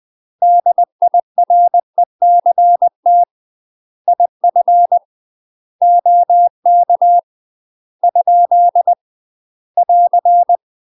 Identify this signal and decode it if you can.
Morse code, heard as a keyed tone, about 20 wpm, 700 Hz. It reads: DIRECT IF OK ? <AR>
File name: UK MASTER QSOs TRAINING / 20WPM / UK MASTER QSOs 0059